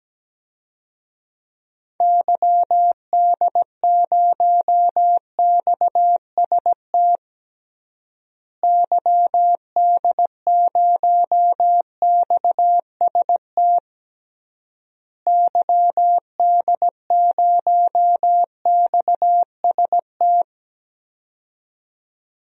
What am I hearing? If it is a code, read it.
Morse code, heard as a keyed tone, about 17 wpm, 700 Hz. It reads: YD0XST YD0XST YD0XST
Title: DXCC-100 Call Signs - 6 Characters - Repeated 3x - 17wpm